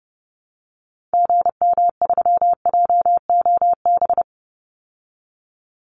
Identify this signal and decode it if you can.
Morse code, heard as a keyed tone, about 30 wpm, 700 Hz. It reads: GM3JOB
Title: Call Signs Repeated 2x - 30wpm